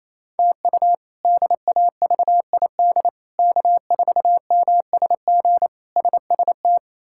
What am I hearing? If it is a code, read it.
Morse code, heard as a keyed tone, about 28 wpm, 700 Hz. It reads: TU DAVID K4MSG SST
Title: K1USN SST QSO Practice - 28wpm